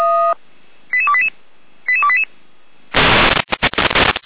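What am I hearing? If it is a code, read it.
Five-tone selective call (ZVEI), heard as quick runs of five tones, tones of about 70 ms. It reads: 88288 88288
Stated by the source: ZVEI five-tone selective call 88289 received with an SDR.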